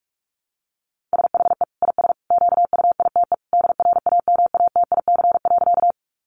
Morse code, heard as a keyed tone, about 44 wpm, 700 Hz. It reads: SHE IS QUITE DRAMATIC.
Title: Sentences from Top 1300 Words 45wpm